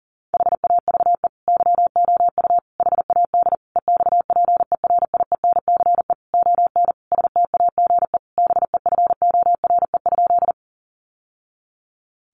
Morse code, heard as a keyed tone, about 40 wpm, 700 Hz. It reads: HAVE YOU HAD EXPERIENCE ON STAGE BEFORE?
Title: Sentences from Top 1200 Words 40wpm